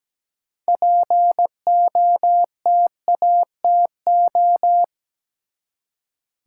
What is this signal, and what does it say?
Morse code, heard as a keyed tone, about 17 wpm, 700 Hz. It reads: POTATO
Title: Six Letter Words - 17wpm